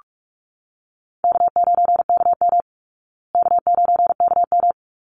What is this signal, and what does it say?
Morse code, heard as a keyed tone, about 44 wpm, 700 Hz. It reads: K9KM K9KM
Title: Easy Call Signs Repeated 2x - 45wpm